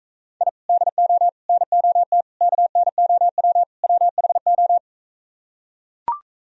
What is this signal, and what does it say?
Morse code, heard as a keyed tone, about 44 wpm, 700 Hz. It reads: I DO NOT KNOW WHO